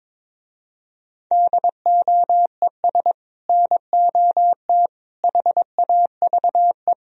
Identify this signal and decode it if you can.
Morse code, heard as a keyed tone, about 22 wpm, 700 Hz. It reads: DOES NOT HAVE
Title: Speed-Racing - Sets of 3 Words 22wpm